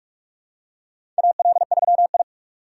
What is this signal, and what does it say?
Morse code, heard as a keyed tone, about 46 wpm, 700 Hz. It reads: AL3I
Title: USA Call Signs - 4 Characters - 4x Character Spacing - 45wpm